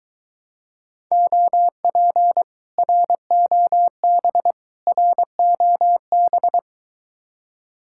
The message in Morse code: OP ROB ROB

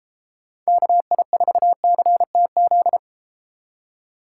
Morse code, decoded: KI4CTZ

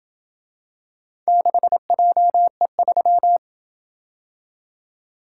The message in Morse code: 6JE3

